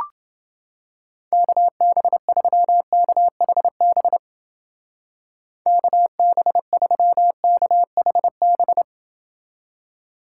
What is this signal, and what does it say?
Morse code, heard as a keyed tone, about 29 wpm, 700 Hz. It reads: KB3KHB KB3KHB